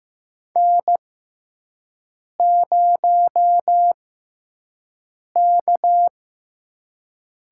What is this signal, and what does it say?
Morse code, heard as a keyed tone, about 15 wpm, spaced as for 5 wpm, 700 Hz. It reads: N0K
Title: USA Call Signs - 5 Characters - 6x Character Spacing - 15wpm